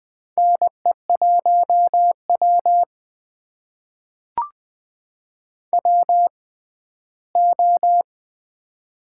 Morse code, heard as a keyed tone, about 20 wpm, spaced as for 14 wpm, 700 Hz. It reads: NE1W WO
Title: USA Call Signs - 4 Characters - 6x Character Spacing - 20wpm